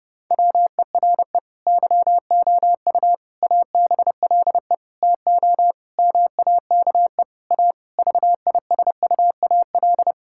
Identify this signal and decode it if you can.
Morse code, heard as a keyed tone, about 30 wpm, 700 Hz. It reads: WERE YOU ABLE TO MAKE A VISUAL
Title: Sentences from Top 1000 Words - Review Part 6 - 30wpm